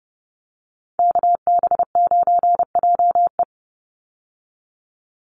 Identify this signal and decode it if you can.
Morse code, heard as a keyed tone, about 30 wpm, 700 Hz. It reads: KB9JE
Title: USA Call Signs 30wpm